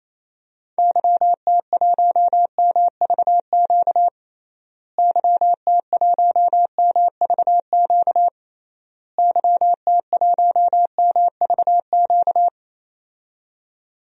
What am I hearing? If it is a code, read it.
Morse code, heard as a keyed tone, about 28 wpm, 700 Hz. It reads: YT1MVQ YT1MVQ YT1MVQ
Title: DXCC-100 Call Signs - 6 Characters - Repeated 3x - 28wpm